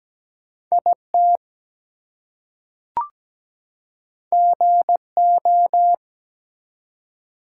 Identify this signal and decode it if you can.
Morse code, heard as a keyed tone, about 17 wpm, 700 Hz. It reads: IT GO
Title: Two Letter Words - 17wpm